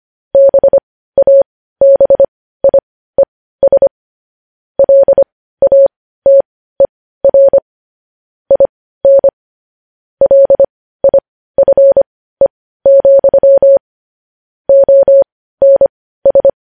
Morse code, heard as a keyed tone, about 25 wpm, spaced as for 15 wpm, 550 Hz. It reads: BABIES LATER IN LIFE, ONS
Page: news headlines in morse code at 15 wpm